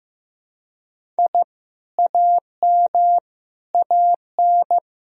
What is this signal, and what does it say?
Morse code, heard as a keyed tone, about 15 wpm, 700 Hz. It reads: I AM AN
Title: Sentences from Top 1000 Words - Review Part 1 - 15wpm